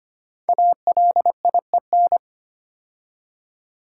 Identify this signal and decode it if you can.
Morse code, heard as a keyed tone, about 25 wpm, 700 Hz. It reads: ALIEN
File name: 5 Letter Words - First 3 Letters Given - 25wpm